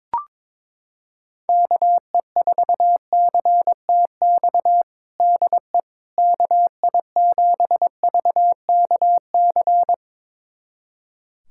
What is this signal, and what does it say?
Morse code, heard as a keyed tone, about 22 wpm, 700 Hz. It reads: KE4CTX DE KI7VKC
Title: USA Callsign DE Callsign - 6 Characters - 22wpm